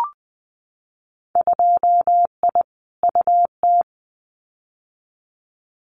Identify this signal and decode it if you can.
Morse code, heard as a keyed tone, about 20 wpm, 700 Hz. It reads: 2I UT
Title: Winter Field Day Exchanges 20wpm